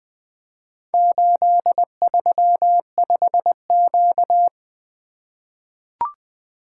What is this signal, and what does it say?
Morse code, heard as a keyed tone, about 20 wpm, 700 Hz. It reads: 835Q
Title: Four Letter-Number Combinations - 20wpm